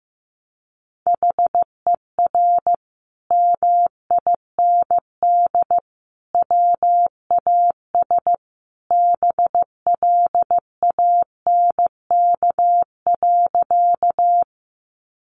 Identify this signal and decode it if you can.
Morse code, heard as a keyed tone, about 15 wpm, 700 Hz. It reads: HER MIND WAS BLANK.
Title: Sentences from Top 1500 Words 15wpm